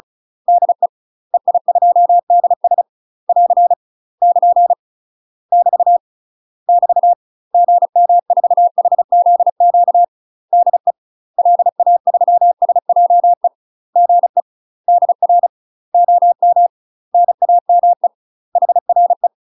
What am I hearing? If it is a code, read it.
Morse code, heard as a keyed tone, about 34 wpm, 700 Hz. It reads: DE EI2DS <AR> <KN> <BT> <BT> GM4HZQ DE LA3SJE GE DR OM NAME HRE
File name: UK MASTER QSOs TRAINING / 35WPM / UK MASTER QSOs 0000